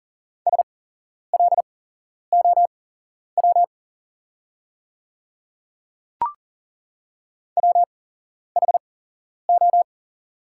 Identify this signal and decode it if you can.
Morse code, heard as a keyed tone, about 40 wpm, spaced as for 11 wpm, 700 Hz. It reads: SLOW WHO
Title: 8x Character Spacing - Top 500 Words - 40wpm